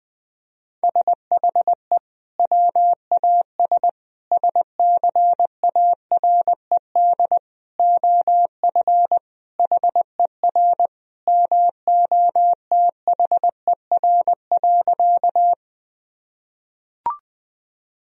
Morse code, heard as a keyed tone, about 20 wpm, 700 Hz. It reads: SHE WAS SCARED OF HER MOTHER.
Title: Sentences from Top 1000 Words - Review Part 4 - 20wpm